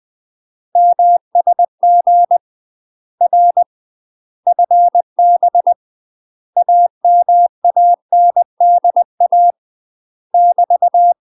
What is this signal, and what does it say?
Morse code, heard as a keyed tone, about 20 wpm, 700 Hz. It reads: MSG R FB AMANDA <BT>
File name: UK MASTER QSOs TRAINING/20WPM